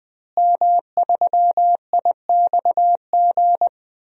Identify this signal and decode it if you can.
Morse code, heard as a keyed tone, about 21 wpm, 700 Hz. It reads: M3IXG
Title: DXCC-100 Call Signs - 5 Characters - Repeated 3x - 20wpm